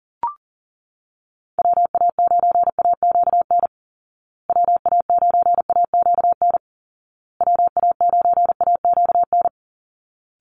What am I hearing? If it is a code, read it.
Morse code, heard as a keyed tone, about 40 wpm, 700 Hz. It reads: WA9AQN WA9AQN WA9AQN
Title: Call Signs Repeated 2x - 40wpm